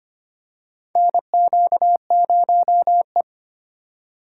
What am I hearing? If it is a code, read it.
Morse code, heard as a keyed tone, about 25 wpm, 700 Hz. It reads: NQ0E
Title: USA Call Signs - 4 Characters - 2x Character Spacing - 25wpm